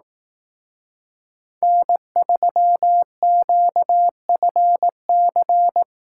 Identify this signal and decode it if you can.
Morse code, heard as a keyed tone, about 18 wpm, 700 Hz. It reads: N3QFC
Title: Speed Racing - USA 1x3 Call Signs 15wpm